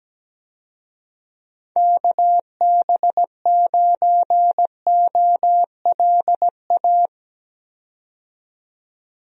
Morse code, heard as a keyed tone, about 17 wpm, 700 Hz. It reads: KB9OLA
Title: USA Call Signs - 6 Characters - 17wpm